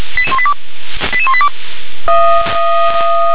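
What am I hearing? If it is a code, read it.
Five-tone selective call (ZVEI), heard as quick runs of five tones, tones of about 70 ms. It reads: 88182
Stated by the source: ZVEI five-tone selective call received on SDR.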